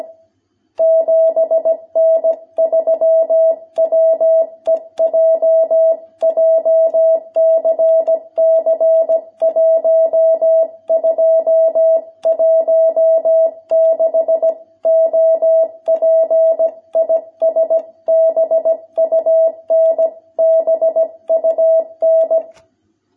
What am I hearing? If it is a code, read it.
Morse code, heard as a keyed tone, about 17 wpm, 660 Hz. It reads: 7N3WEJ JCC 1216 OP IS BUN BUN
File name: jh8cbh / 2021cw